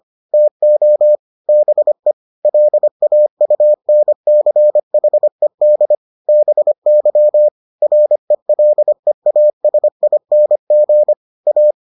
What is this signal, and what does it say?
Morse code, heard as a keyed tone, about 25 wpm, 600 Hz. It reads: TO BE LAUNCHED BY RELEASING A